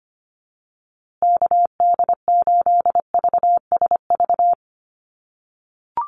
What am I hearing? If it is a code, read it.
Morse code, heard as a keyed tone, about 25 wpm, 700 Hz. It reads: KD8VSV
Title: USA Call Signs Repeated 3x 25wpm